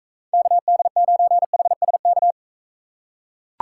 Morse code, heard as a keyed tone, about 43 wpm, 700 Hz. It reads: KD9HSK